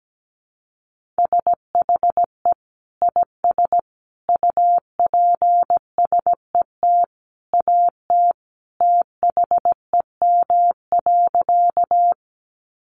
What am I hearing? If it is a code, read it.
Morse code, heard as a keyed tone, about 17 wpm, 700 Hz. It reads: SHE IS UPSET AT THEM.
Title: Sentences from Top 800 Words 17wpm